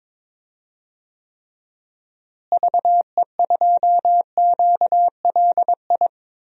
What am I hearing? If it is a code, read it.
Morse code, heard as a keyed tone, about 22 wpm, 700 Hz. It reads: VE2QLI
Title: DXCC-100 Call Signs - 6 Characters - Repeated 3x - 22wpm